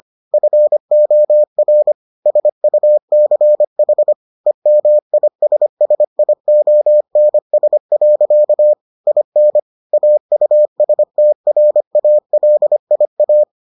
Morse code, heard as a keyed tone, about 25 wpm, 600 Hz. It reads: FOR SUCH EMISSIONS. IN AUSTRALIA